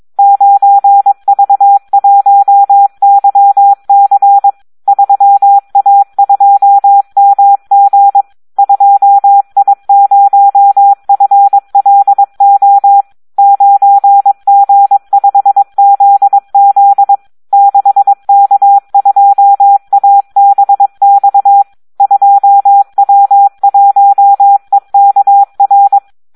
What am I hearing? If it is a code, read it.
Morse code, heard as a keyed tone, about 22 wpm, 800 Hz, with radio noise. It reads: 9V1YC 3A2MG 2I0FLO 9G5ZZ 6K2ABX 2W1EKR
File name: Audio practica CW / SD 1 22